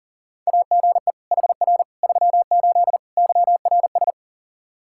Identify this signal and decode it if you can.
Morse code, heard as a keyed tone, about 40 wpm, 700 Hz. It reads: AGE HR 38 YRS
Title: K1USN SST QSO Practice 40wpm